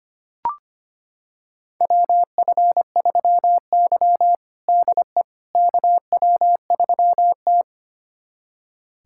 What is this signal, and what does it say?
Morse code, heard as a keyed tone, about 25 wpm, 700 Hz. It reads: WF3Y DE KW3T
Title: USA Callsign DE Callsign - 4 Characters - 25wpm